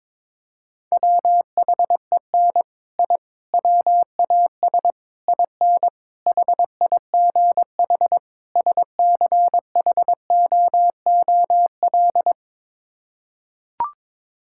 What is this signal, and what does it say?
Morse code, heard as a keyed tone, about 21 wpm, 700 Hz. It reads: WHEN I WAS IN HIGH SCHOOL
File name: practice / 22wpm / Sets of 6 Words - 22wpm